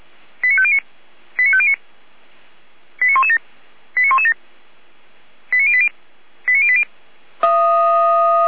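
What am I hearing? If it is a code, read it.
Five-tone selective call (ZVEI), heard as quick runs of five tones, tones of about 70 ms. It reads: 89509 89509 89117 89117 89080 89080
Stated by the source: Five-tone ZVEI selective call received on an SDR.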